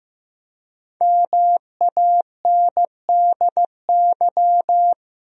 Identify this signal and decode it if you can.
Morse code, heard as a keyed tone, about 15 wpm, 700 Hz. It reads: MANDY
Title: USA Common Names - 15wpm